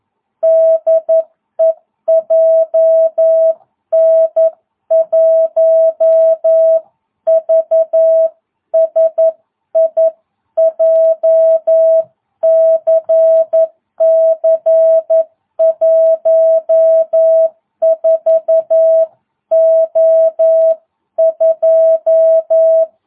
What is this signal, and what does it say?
Morse code, heard as a keyed tone, about 11 wpm, 660 Hz. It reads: DE JN1VSI JCC 14O2